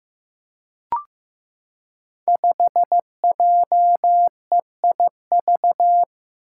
Morse code, heard as a keyed tone, about 15 wpm, 700 Hz. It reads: 5JEIV
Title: Five Challenging Letter-Number Combinations 15wpm